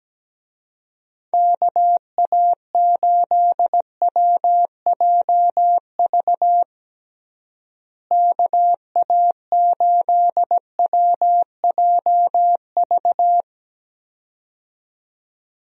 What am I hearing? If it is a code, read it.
Morse code, heard as a keyed tone, about 17 wpm, 700 Hz. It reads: KA8WJV KA8WJV
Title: USA Call Signs - 6 Characters - Repeated 2x - 17wpm